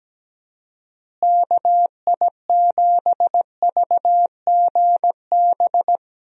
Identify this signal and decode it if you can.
Morse code, heard as a keyed tone, about 17 wpm, 700 Hz. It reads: KI7VGB
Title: USA Call Signs - 6 Characters - Repeated 2x - 17wpm